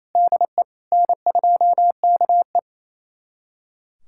Morse code, heard as a keyed tone, about 27 wpm, 700 Hz. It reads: DE N2KE
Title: USA Callsign DE Callsign - 4 Characters - 28wpm